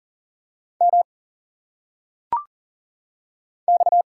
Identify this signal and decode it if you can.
Morse code, heard as a keyed tone, about 40 wpm, 700 Hz. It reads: M X
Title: Single Letters - Phonetic Pronunciation 40wpm